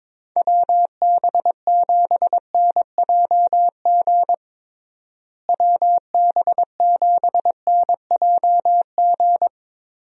Morse code, heard as a keyed tone, about 22 wpm, 700 Hz. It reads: WB7NJG WB7NJG
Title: USA Call Signs - 6 Characters - Repeated 3x - 22wpm